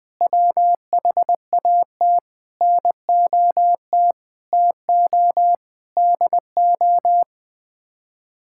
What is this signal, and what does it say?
Morse code, heard as a keyed tone, about 20 wpm, 700 Hz. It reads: WHAT NOT TO DO
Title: Sets of 8 Words - 20wpm